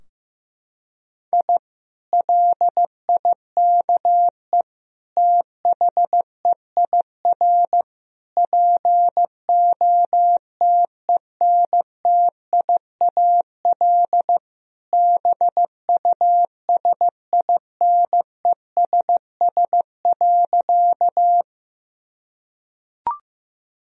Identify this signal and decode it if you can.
Morse code, heard as a keyed tone, about 15 wpm, 700 Hz. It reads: I LIKE THEIR POTENTIAL BUSINESS.
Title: Sentences from Top 1000 Words - Review Part 4 - 15wpm